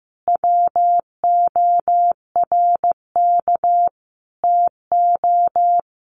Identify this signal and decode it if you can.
Morse code, heard as a keyed tone, about 15 wpm, 700 Hz. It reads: WORK TO
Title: Sets of 5 Words Encore2 15wpm